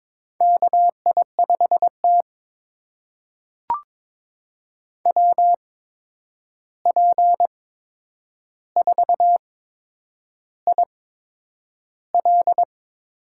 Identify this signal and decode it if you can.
Morse code, heard as a keyed tone, about 22 wpm, spaced as for 9 wpm, 700 Hz. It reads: KI5T WP4IL
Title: USA Call Signs - 8x Character Spacing - 22wpm